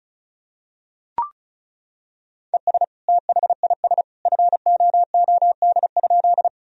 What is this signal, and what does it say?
Morse code, heard as a keyed tone, about 34 wpm, 700 Hz. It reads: ES THIS FOOD?
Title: Sentences from Top 300 Words and Abbreviations 35wpm